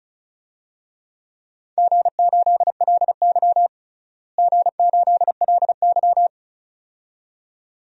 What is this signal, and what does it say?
Morse code, heard as a keyed tone, about 36 wpm, 700 Hz. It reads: G8LY G8LY
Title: DXCC-100 Call Signs - 4 Characters - Repeated 2x - 35wpm